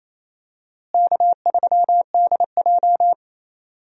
Morse code, heard as a keyed tone, about 28 wpm, 700 Hz. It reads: K3DJ
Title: USA Call Signs - 4 Characters - Repeated 2x - 28wpm